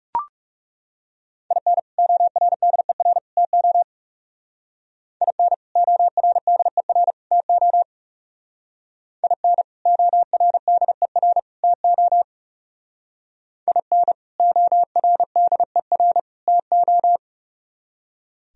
Morse code, heard as a keyed tone, about 38 wpm, 690 Hz. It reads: IN ORDER TO IN ORDER TO IN ORDER TO IN ORDER TO